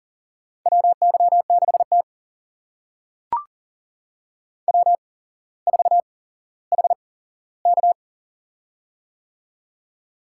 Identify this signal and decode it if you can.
Morse code, heard as a keyed tone, about 40 wpm, spaced as for 17 wpm, 700 Hz. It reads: WY6T W4HK
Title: USA Call Signs - 4 Characters - 8x Character Spacing - 40wpm